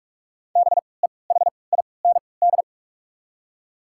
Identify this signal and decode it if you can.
Morse code, heard as a keyed tone, about 42 wpm, spaced as for 25 wpm, 690 Hz. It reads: BEHIND
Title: 3x Character Spacing - Top 500 Words - 45wpm